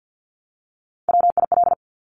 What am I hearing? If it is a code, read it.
Morse code, heard as a keyed tone, about 50 wpm, 700 Hz. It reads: WIL
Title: Top Three Letter Combinations in Words 50wpm